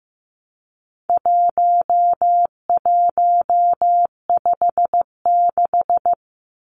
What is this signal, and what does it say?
Morse code, heard as a keyed tone, about 15 wpm, 700 Hz. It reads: 1156
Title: Numbers - 4-Digit 15wpm